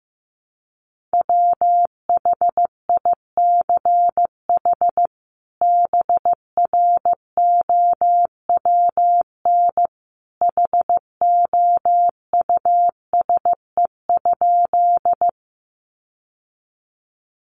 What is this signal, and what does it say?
Morse code, heard as a keyed tone, about 15 wpm, 700 Hz. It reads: WHICH BROWN HOUSE?
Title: Sentences from Top 1200 Words 15wpm